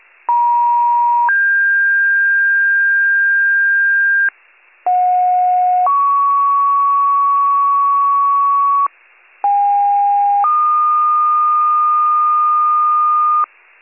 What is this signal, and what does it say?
Two-tone page, heard as a one-second tone, then a three-second tone, about 950 then 1600 Hz, about 730 then 1120 Hz, about 810 then 1220 Hz. It is heard from 0.3 to 13.4 s.